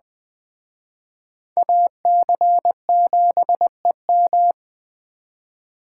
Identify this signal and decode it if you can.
Morse code, heard as a keyed tone, about 20 wpm, 700 Hz. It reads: AC7EM